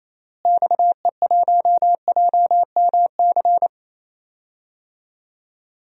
Morse code, heard as a keyed tone, about 27 wpm, 700 Hz. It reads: XE1JMC
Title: DXCC-100 Call Signs - 6 Characters - Repeated 2x - 28wpm